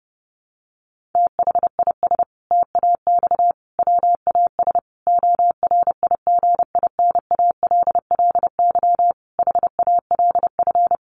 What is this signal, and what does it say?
Morse code, heard as a keyed tone, about 30 wpm, 700 Hz. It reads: THIS TAX WAS ORIGINALLY HALF
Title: Sentences from Top 1500 Words 30wpm